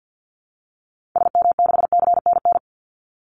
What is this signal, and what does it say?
Morse code, heard as a keyed tone, about 50 wpm, 700 Hz. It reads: SM6CNN
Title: Call Signs - 50wpm